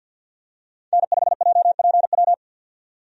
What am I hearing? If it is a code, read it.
Morse code, heard as a keyed tone, about 52 wpm, 700 Hz. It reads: N5JPW
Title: USA Call Signs - 5 Characters - 50wpm